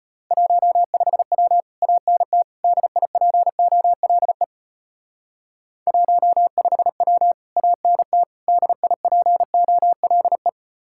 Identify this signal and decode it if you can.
Morse code, heard as a keyed tone, about 33 wpm, 700 Hz. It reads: 15W ANT DIPOLE 15W ANT DIPOLE